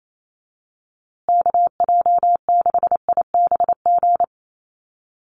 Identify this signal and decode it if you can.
Morse code, heard as a keyed tone, about 28 wpm, 700 Hz. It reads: KJ6IBG
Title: USA Call Signs Repeated 2x 28wpm